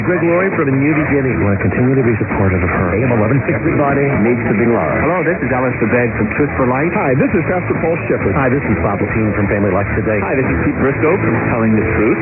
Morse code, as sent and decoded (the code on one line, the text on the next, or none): none